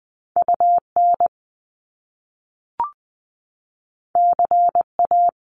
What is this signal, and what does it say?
Morse code, heard as a keyed tone, about 19 wpm, 700 Hz. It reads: UN CA
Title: Top Two Letter Combinations in Words 20wpm